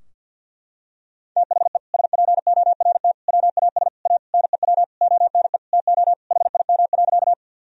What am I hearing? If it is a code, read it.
Morse code, heard as a keyed tone, about 50 wpm, 700 Hz. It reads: THE SPORT WAS A NEW ONE TO HIM.
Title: Sentences from Top 1000 Words - Review Part 4 - 50wpm